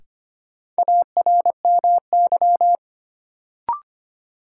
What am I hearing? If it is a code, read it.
Morse code, heard as a keyed tone, about 25 wpm, 700 Hz. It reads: ARMY
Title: Sentences from Top 800 Words - New Word Drill - 25wpm